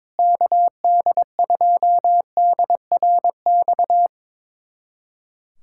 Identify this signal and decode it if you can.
Morse code, heard as a keyed tone, about 22 wpm, 700 Hz. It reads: KD2DRX
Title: USA Callsign DE Callsign - 6 Characters - 22wpm